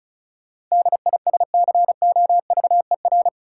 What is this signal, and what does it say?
Morse code, heard as a keyed tone, about 36 wpm, 700 Hz. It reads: DISCOVER
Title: Sentences from Top 1100 Words - New Word Drill - 35wpm